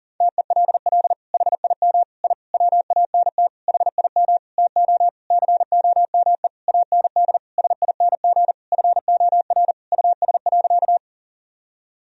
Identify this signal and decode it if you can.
Morse code, heard as a keyed tone, about 37 wpm, 700 Hz. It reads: TELL HIM I WANT HIM TO COME AND SING FOR US.
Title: Sentences from Top 1000 Words - Review Part 2 - 40wpm